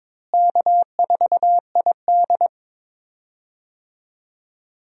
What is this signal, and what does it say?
Morse code, heard as a keyed tone, about 22 wpm, 700 Hz. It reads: K4ID
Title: USA Call Signs - 4 Characters - Repeated 2x - 22wpm